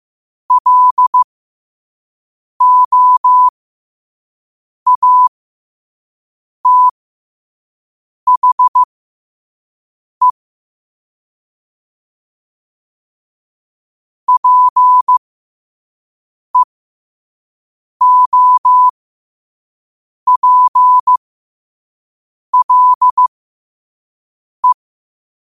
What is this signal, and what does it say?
Morse code, heard as a keyed tone, about 15 wpm, spaced as for 5 wpm, 1000 Hz. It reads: LOATHE PEOPLE
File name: qotdfiles / 5wpm